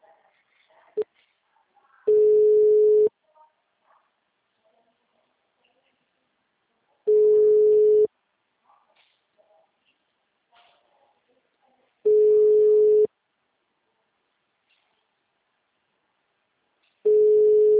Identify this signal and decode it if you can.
Ringback tone (Europe, 425 Hz), with speech.